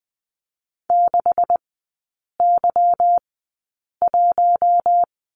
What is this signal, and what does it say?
Morse code, heard as a keyed tone, about 20 wpm, spaced as for 8 wpm, 690 Hz. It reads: 6Y1
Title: 145 Lesson 24e - Letter y - Three Characters 20wpm